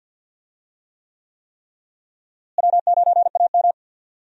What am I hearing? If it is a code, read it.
Morse code, heard as a keyed tone, about 53 wpm, 700 Hz. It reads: W9AM
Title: USA Call Signs - 4 Characters - Repeated 3x - 50wpm